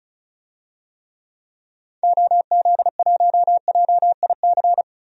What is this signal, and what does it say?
Morse code, heard as a keyed tone, about 36 wpm, 700 Hz. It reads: OZ1JIC
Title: DXCC-100 Call Signs - 6 Characters - 35wpm